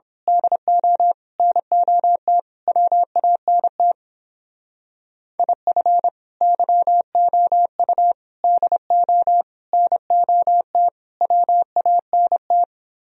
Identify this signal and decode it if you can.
Morse code, heard as a keyed tone, about 26 wpm, 700 Hz. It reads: DO NOT WANT IF YOU DO NOT WANT
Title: Speed-Racing - Sets of 5 Words 22wpm